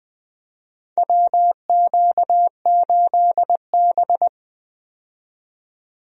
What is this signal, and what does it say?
Morse code, heard as a keyed tone, about 21 wpm, 700 Hz. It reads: WQ8B
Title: USA Call Signs - 4 Characters - 20wpm